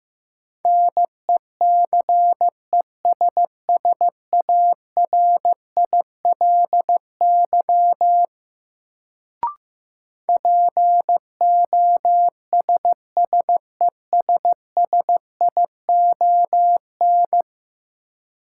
Morse code, heard as a keyed tone, about 15 wpm, 700 Hz. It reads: NECESSARILY POSSESSION
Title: Sentences from Top 1400 Words - New Word Drill - 15wpm